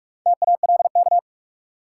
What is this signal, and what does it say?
Morse code, heard as a keyed tone, about 44 wpm, 700 Hz. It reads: TALK
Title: Sentences from Top 400 Words - New Word Drill - 45wpm